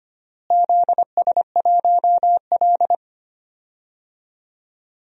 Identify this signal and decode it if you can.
Morse code, heard as a keyed tone, about 25 wpm, 700 Hz. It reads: ZS1L
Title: DXCC-100 Call Signs - 4 Characters - Repeated 2x - 25wpm